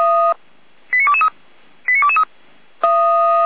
Five-tone selective call (ZVEI), heard as quick runs of five tones, tones of about 80 ms. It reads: 89223 89223